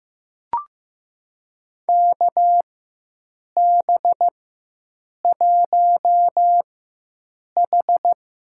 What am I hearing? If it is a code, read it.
Morse code, heard as a keyed tone, about 16 wpm, spaced as for 7 wpm, 700 Hz. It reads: KB1H
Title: USA Call Signs - 4x Character Spacing - 15wpm